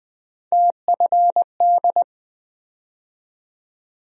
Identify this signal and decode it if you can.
Morse code, heard as a keyed tone, about 20 wpm, 700 Hz. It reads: TFD
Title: USA Call Sign Postfixes - 3 Letters 20wpm